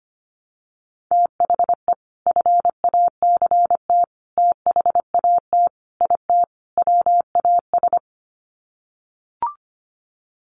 Morse code, heard as a keyed tone, about 25 wpm, 700 Hz. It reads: THE FACT THAT IT WAS
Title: Sets of 5 Words Encore2 25wpm